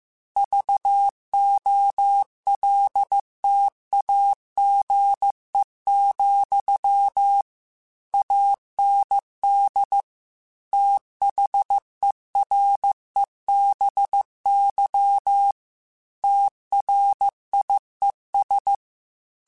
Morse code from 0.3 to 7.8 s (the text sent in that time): VOLTAGE,